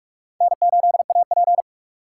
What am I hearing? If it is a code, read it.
Morse code, heard as a keyed tone, about 49 wpm, 700 Hz. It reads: N8AP